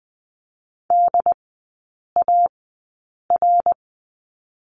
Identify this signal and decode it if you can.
Morse code, heard as a keyed tone, about 20 wpm, spaced as for 8 wpm, 700 Hz. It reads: DAR
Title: 074 Lesson 13e - Letter L - Three Characters 20wpm